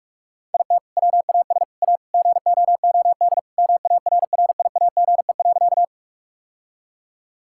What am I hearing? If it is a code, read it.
Morse code, heard as a keyed tone, about 42 wpm, 700 Hz. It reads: IT WAS A GOOD MARRIAGE.
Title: Sentences from Top 1000 Words - Review Part 5 - 45wpm